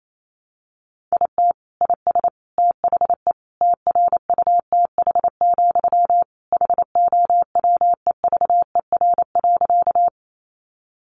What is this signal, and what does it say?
Morse code, heard as a keyed tone, about 28 wpm, 700 Hz. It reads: IT IS THE TRUTH, HOWEVER.
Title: Sentences from Top 800 Words 28wpm